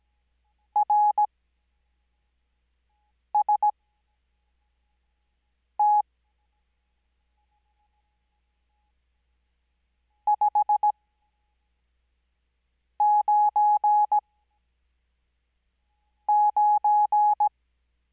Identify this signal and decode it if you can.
Morse code, heard as a keyed tone, about 17 wpm, spaced as for 4 wpm, 840 Hz. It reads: RST 599
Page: Farnsworth Spacing: 15WPM Character speed Spaced at 5WPM.